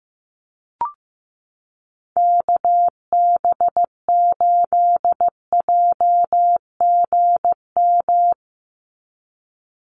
Morse code, heard as a keyed tone, about 15 wpm, 700 Hz. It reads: KB8JGM
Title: USA Call Signs 15wpm